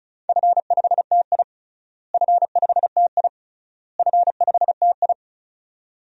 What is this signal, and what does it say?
Morse code, heard as a keyed tone, about 36 wpm, 700 Hz. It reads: F5TI F5TI F5TI
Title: DXCC-100 Call Signs - 4 Characters - Repeated 3x - 35wpm